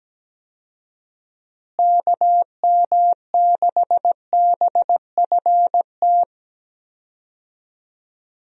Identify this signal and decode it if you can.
Morse code, heard as a keyed tone, about 17 wpm, 700 Hz. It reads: KM6BFT
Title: USA Call Signs - 6 Characters - Repeated 3x - 17wpm